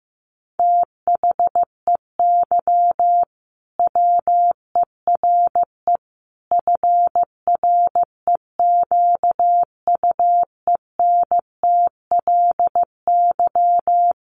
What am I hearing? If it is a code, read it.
Morse code, heard as a keyed tone, about 14 wpm, 700 Hz. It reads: THEY WERE FREQUENTLY
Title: Sentences from Top 1400 Words 15wpm